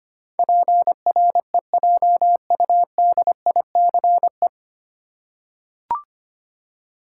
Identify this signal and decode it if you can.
Morse code, heard as a keyed tone, about 25 wpm, 700 Hz. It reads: PREJUDICE
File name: practice / 25wpm / 9 Letter Words - First 3 Letters Given - 25wpm